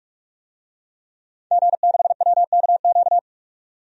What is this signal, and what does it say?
Morse code, heard as a keyed tone, about 46 wpm, 700 Hz. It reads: G6WKQ
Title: DXCC-100 Call Signs - 5 Characters - 45wpm